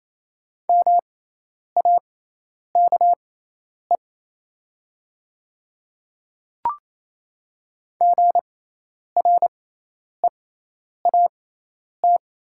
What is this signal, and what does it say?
Morse code, heard as a keyed tone, about 28 wpm, spaced as for 10 wpm, 700 Hz. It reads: MAKE GREAT
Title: 6x Character Spacing - Top 300 Words - 28wpm